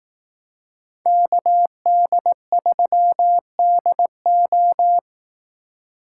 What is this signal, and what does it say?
Morse code, heard as a keyed tone, about 18 wpm, 700 Hz. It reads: KD3DO
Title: Speed Racing - USA Call Signs 15wpm